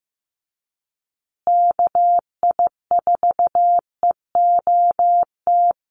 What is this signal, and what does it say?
Morse code, heard as a keyed tone, about 15 wpm, 700 Hz. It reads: KI4EOT
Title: USA Call Signs Repeated 3x 15wpm